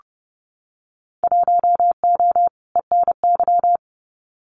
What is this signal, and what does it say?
Morse code, heard as a keyed tone, about 31 wpm, 700 Hz. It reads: 1O ENY
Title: Winter Field Day Exchanges 30wpm